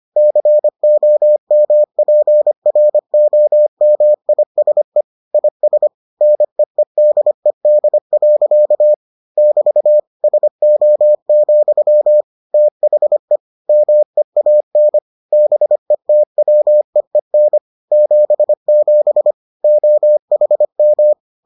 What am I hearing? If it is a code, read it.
Morse code, heard as a keyed tone, about 25 wpm, 600 Hz. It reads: COMPROMISE IS NEEDED. <BT> SO, THE MEAN BETWEEN 77 OHM